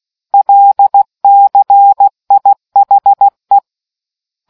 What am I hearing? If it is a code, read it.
Morse code, heard as a keyed tone, about 16 wpm, 800 Hz. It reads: LCIHE